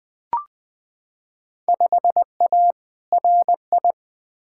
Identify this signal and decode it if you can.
Morse code, heard as a keyed tone, about 20 wpm, 700 Hz. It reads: 5A RI
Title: Field Day Exchanges Encore - 20wpm